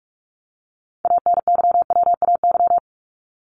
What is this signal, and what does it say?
Morse code, heard as a keyed tone, about 44 wpm, 700 Hz. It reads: ANYWAY
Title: Top 1501-1750 Words 45wpm